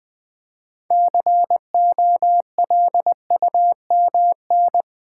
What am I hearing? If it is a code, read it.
Morse code, heard as a keyed tone, about 20 wpm, 700 Hz. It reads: COLUMN
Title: Build A Word - 3 Letters At A Time - 6-Letter Words - 20wpm